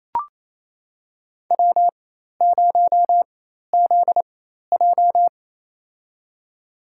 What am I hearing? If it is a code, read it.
Morse code, heard as a keyed tone, about 28 wpm, spaced as for 13 wpm, 700 Hz. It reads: W0ZJ
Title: USA Call Signs - 4x Character Spacing - 28wpm